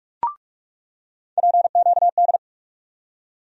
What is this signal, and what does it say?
Morse code, heard as a keyed tone, about 48 wpm, 700 Hz. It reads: PQD
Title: USA Call Sign Postfixes - 3 Letters 45wpm